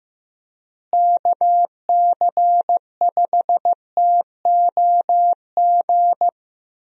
Morse code, heard as a keyed tone, about 15 wpm, 700 Hz. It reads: KC5TOG
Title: USA Call Signs - 6 Characters - Repeated 2x - 15wpm